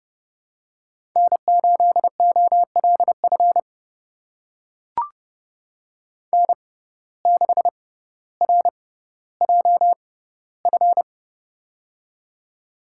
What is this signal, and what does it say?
Morse code, heard as a keyed tone, about 31 wpm, spaced as for 15 wpm, 700 Hz. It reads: N8OLF N6RJF
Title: USA Call Signs - 5 Characters - 6x Character Spacing - 30wpm